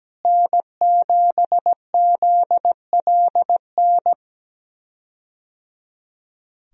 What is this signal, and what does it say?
Morse code, heard as a keyed tone, about 17 wpm, 700 Hz. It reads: N7ZLN
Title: USA Call Signs - 5 Characters - Repeated 3x - 17wpm